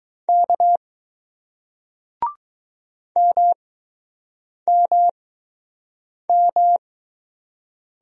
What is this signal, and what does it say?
Morse code, heard as a keyed tone, about 20 wpm, spaced as for 6 wpm, 700 Hz. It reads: K MMM